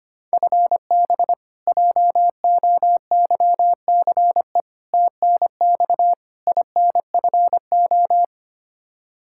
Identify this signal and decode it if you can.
Morse code, heard as a keyed tone, about 25 wpm, 700 Hz. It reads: FB JOYCE TNX INFO
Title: K1USN SST QSO Practice 25wpm